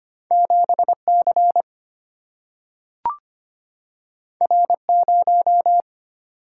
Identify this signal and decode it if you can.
Morse code, heard as a keyed tone, about 25 wpm, 700 Hz. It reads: 7C R0
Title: Single-Letter Single-Number 25wpm